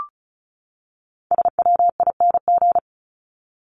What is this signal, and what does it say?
Morse code, heard as a keyed tone, about 34 wpm, 700 Hz. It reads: SWING
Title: Top 1751-2000 Words 35wpm